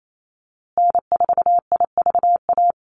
Morse code, heard as a keyed tone, about 28 wpm, 700 Hz. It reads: N4IVA
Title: USA Call Signs Repeated 2x 28wpm